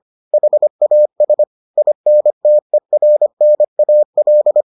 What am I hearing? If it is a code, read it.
Morse code, heard as a keyed tone, about 25 wpm, 600 Hz. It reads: HAS INTERNAL